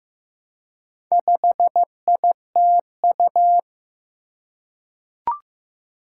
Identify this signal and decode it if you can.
Morse code, heard as a keyed tone, about 15 wpm, 700 Hz. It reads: 5ITU
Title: Four Challenging Letter-Number Combinations 15wpm